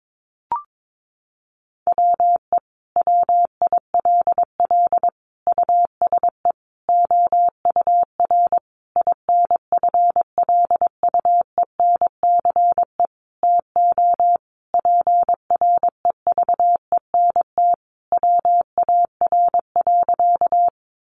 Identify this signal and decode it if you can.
Morse code, heard as a keyed tone, about 22 wpm, 700 Hz. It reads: WE WILL USE OUR INFLUENCE TO PREVENT WAR.
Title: Sentences from Top 1100 Words 22wpm